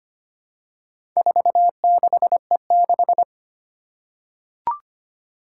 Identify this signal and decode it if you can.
Morse code, heard as a keyed tone, about 25 wpm, 700 Hz. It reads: 46E6
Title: Four Challenging Letter-Number Combinations 25wpm